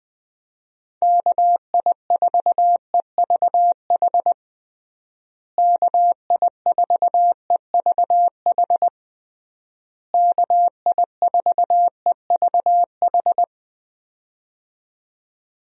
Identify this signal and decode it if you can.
Morse code, heard as a keyed tone, about 20 wpm, 700 Hz. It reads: KI4EVH KI4EVH KI4EVH
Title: USA Call Signs - 6 Characters - Repeated 3x - 20wpm